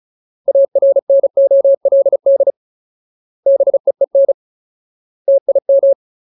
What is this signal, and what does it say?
Morse code, heard as a keyed tone, about 34 wpm, 550 Hz. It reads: ARNOLD BEEN TIM
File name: Ultimate Word List / 35WPM Ultimate